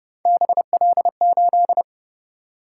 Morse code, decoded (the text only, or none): BL8